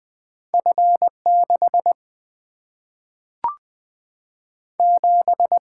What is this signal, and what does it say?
Morse code, heard as a keyed tone, about 20 wpm, 700 Hz. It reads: F 6 7
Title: Single-Letter Single-Number 20wpm